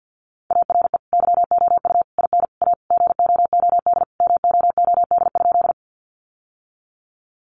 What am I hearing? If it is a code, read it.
Morse code, heard as a keyed tone, about 49 wpm, 700 Hz. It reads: ARE YOU IN A GOOD MOOD?
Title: Sentences from Top 1100 Words 50wpm